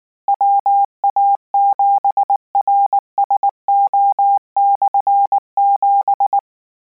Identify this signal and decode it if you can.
Morse code, heard as a keyed tone, about 19 wpm, 800 Hz. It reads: WA7RSO/7